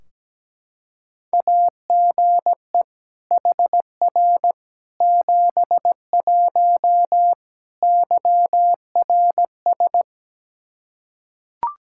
Morse code, heard as a keyed tone, about 17 wpm, 700 Hz. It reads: AGE HR 71 YRS
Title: Practice QSO - Exchange 3 - More Info - 17wpm